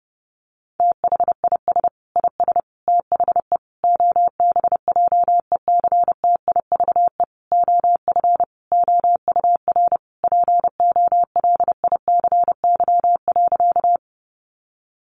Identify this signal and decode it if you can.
Morse code, heard as a keyed tone, about 28 wpm, 700 Hz. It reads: THIS IS THE OBJECTIVE OF OUR POLICY.
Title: Sentences from Top 1200 Words 30wpm